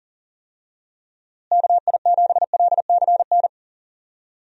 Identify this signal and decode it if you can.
Morse code, heard as a keyed tone, about 40 wpm, 700 Hz. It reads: KI7LCN
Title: USA Call Signs - 6 Characters - Repeated 2x - 40wpm